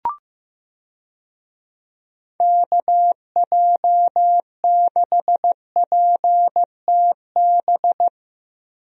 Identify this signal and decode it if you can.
Morse code, heard as a keyed tone, about 15 wpm, 700 Hz. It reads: KJ6PTB
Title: USA Call Signs - 6 Characters - Repeated 3x - 15wpm